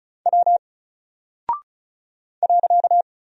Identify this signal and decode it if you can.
Morse code, heard as a keyed tone, about 36 wpm, 700 Hz. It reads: W .
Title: Single Character (Letter:Number:Punctuation:Prosign) Alt 35wpm